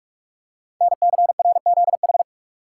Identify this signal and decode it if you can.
Morse code, heard as a keyed tone, about 44 wpm, 700 Hz. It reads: NCRZH
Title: Code Groups - 5 Letters - 45wpm